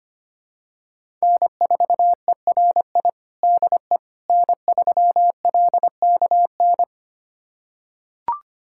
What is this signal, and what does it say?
Morse code, heard as a keyed tone, about 25 wpm, 700 Hz. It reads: N4ERI DE N3LKN
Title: USA Callsign DE Callsign - 5 Characters - 25wpm